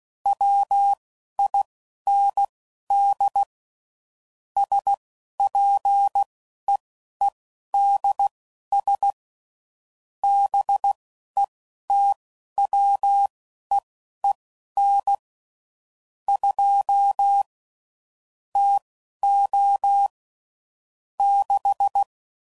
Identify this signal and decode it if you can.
Morse code, heard as a keyed tone, about 16 wpm, spaced as for 11 wpm, 800 Hz. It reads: WIND SPEEDS BETWEEN 2 TO 6
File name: CW Code Groups @10WPM / Weather Sets / Weather 4 10